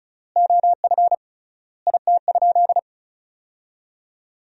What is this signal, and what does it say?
Morse code, heard as a keyed tone, about 36 wpm, 700 Hz. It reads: OF IT?
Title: 3x Word-Spacing - Sentences from Top 100 Words - 35wpm